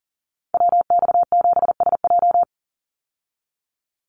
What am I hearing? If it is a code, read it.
Morse code, heard as a keyed tone, about 40 wpm, 700 Hz. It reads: WX7SJ
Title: Call Signs Repeated 2x - 40wpm